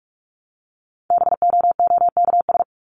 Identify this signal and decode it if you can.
Morse code, heard as a keyed tone, about 44 wpm, 700 Hz. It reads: BOOKS
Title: ES:S Suffix - Top 100 Words with Base Words 45wpm